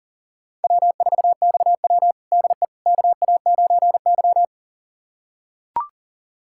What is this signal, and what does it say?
Morse code, heard as a keyed tone, about 40 wpm, 700 Hz. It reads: W4XW DE KA9Y
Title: USA Callsign DE Callsign - 4 Characters - 40wpm